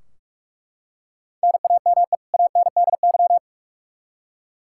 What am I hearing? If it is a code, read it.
Morse code, heard as a keyed tone, about 42 wpm, 700 Hz. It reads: NAME ANDY
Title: Practice QSO - Exchange 1 - Beginning - 45wpm